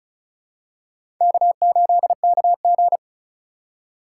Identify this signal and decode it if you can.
Morse code, heard as a keyed tone, about 36 wpm, 700 Hz. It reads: K8KG
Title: USA Call Signs - 4 Characters - 4x Character Spacing - 35wpm